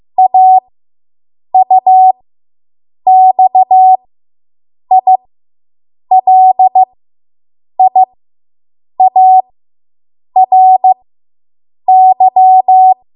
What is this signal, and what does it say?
Morse code, heard as a keyed tone, about 15 wpm, spaced as for 7 wpm, 750 Hz. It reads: AUXILIARY